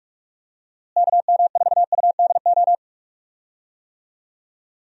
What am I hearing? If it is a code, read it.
Morse code, heard as a keyed tone, about 46 wpm, 700 Hz. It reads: KM4UDO
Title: USA Call Signs - 6 Characters - Repeated 2x - 45wpm